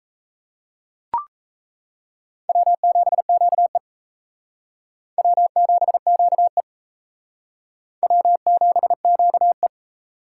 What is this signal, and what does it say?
Morse code, heard as a keyed tone, about 35 wpm, 700 Hz. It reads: W7QE W7QE W7QE
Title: Speed Racing - USA 1x2 Call Signs 28wpm